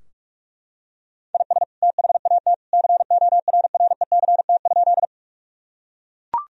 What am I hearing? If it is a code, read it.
Morse code, heard as a keyed tone, about 42 wpm, 700 Hz. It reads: IS THAT CORRECT?
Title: Sentences from Top 1000 Words - Review Part 2 - 45wpm